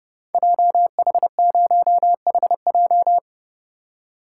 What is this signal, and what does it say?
Morse code, heard as a keyed tone, about 30 wpm, 700 Hz. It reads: JH0HJ
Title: Code Groups - 5 Letter:Numbers - 30wpm